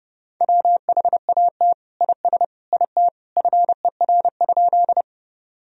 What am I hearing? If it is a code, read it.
Morse code, heard as a keyed tone, about 30 wpm, 700 Hz. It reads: WHAT IS IT FER?
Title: Sentences from Top 100 Words and Abbreviations 30wpm